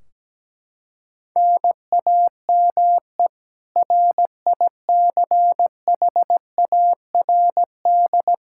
Morse code, heard as a keyed tone, about 17 wpm, 700 Hz. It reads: NAME RICHARD
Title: Practice QSO - Exchange 1 - Beginning - 17wpm